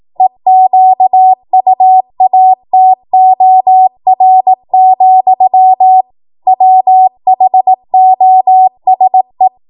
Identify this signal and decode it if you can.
Morse code, heard as a keyed tone, about 18 wpm, 750 Hz. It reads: EQUATOR, WHOSE
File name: Morse / 18WPM